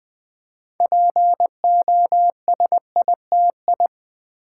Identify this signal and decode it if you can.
Morse code, heard as a keyed tone, about 20 wpm, 700 Hz. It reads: POSITI
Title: Build A Word - 3 Letters At A Time - 8-Letter Words - 20wpm